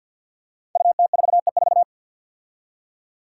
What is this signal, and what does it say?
Morse code, heard as a keyed tone, about 48 wpm, 700 Hz. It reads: UT4E4
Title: Five Challenging Letter-Number Combinations 50wpm